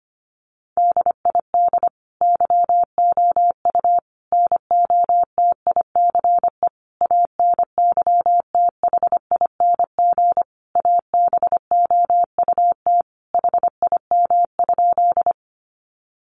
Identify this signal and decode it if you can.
Morse code, heard as a keyed tone, about 25 wpm, 700 Hz. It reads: DID YOU NOTICE ANYTHING ABOUT HIM?
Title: Sentences from Top 1000 Words 25wpm